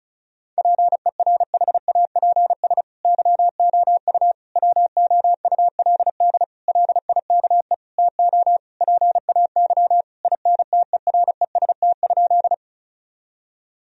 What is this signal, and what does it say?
Morse code, heard as a keyed tone, about 36 wpm, 700 Hz. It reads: PERHAPS YOU WOULD LIKE TO PAY INTEREST?
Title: Sentences from Top 1000 Words - Review Part 2 - 35wpm